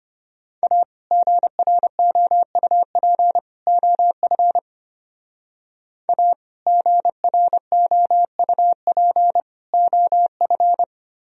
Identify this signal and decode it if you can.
Morse code, heard as a keyed tone, about 30 wpm, 700 Hz. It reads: A GROUP OF A GROUP OF
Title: Speed-Racing - Sets of 3 Words 25wpm